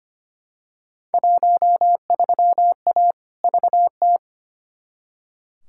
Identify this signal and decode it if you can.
Morse code, heard as a keyed tone, about 25 wpm, 700 Hz. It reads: 13A VT
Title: Field Day Exchanges Encore - 25wpm